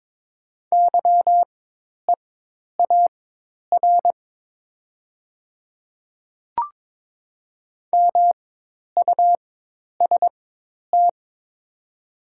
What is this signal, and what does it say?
Morse code, heard as a keyed tone, about 22 wpm, spaced as for 10 wpm, 700 Hz. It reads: YEAR MUST
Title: 4x Character Spacing - Top 200 Words - 22wpm